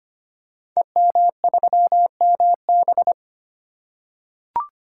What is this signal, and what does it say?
Morse code, heard as a keyed tone, about 25 wpm, 700 Hz. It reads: EM3MB
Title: Code Groups - 5 Letter:Numbers - 25wpm